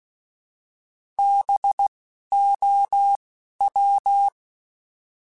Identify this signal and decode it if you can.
Morse code, heard as a keyed tone, about 16 wpm, spaced as for 11 wpm, 800 Hz. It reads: BOW
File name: Alphabetical / Alphabetical mp3s / Bs @10